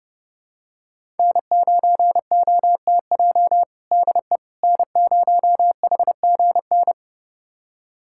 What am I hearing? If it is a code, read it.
Morse code, heard as a keyed tone, about 30 wpm, 700 Hz. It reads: N9OTJ DE N0HGN
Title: USA Callsign DE Callsign - 5 Characters - 30wpm